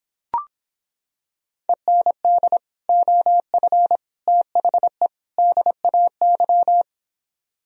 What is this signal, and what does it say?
Morse code, heard as a keyed tone, about 26 wpm, 700 Hz. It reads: END OF THE DAY